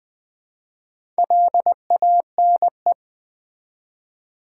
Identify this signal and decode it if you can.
Morse code, heard as a keyed tone, about 20 wpm, 700 Hz. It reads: LANE
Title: Build A Word - 2 Letters At A Time - 4-Letter Words - 20wpm